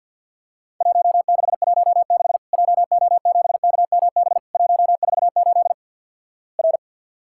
Morse code, heard as a keyed tone, about 50 wpm, 700 Hz. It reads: 1616 JO7KMB 148 R
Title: Worked All Europe DX Contest - QTC Exchange - 50wpm